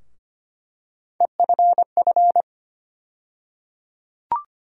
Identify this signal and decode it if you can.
Morse code, heard as a keyed tone, about 25 wpm, 700 Hz. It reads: EFF
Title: USA Call Sign Postfixes - 3 Letters 25wpm